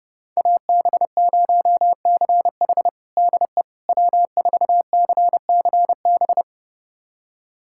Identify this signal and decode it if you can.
Morse code, heard as a keyed tone, about 32 wpm, 700 Hz. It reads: AB0CH DE W4CCB
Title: USA Callsign DE Callsign - 5 Characters - 30wpm